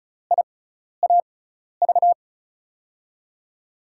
Morse code, heard as a keyed tone, about 36 wpm, spaced as for 12 wpm, 690 Hz. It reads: IAV
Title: USA Call Signs - 6 Characters - 6x Character Spacing - 35wpm